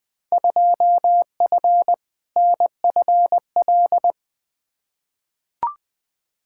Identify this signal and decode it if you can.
Morse code, heard as a keyed tone, about 20 wpm, 700 Hz. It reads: 2F NFL
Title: Field Day Exchanges - 20wpm